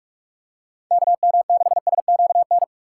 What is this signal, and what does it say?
Morse code, heard as a keyed tone, about 46 wpm, 700 Hz. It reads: KM6SQN
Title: USA Call Signs - 4x Character Spacing - 45wpm